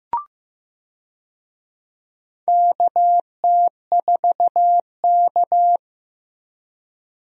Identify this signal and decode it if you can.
Morse code, heard as a keyed tone, about 15 wpm, 700 Hz. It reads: KT4K
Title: USA Call Signs - 4 Characters - 15wpm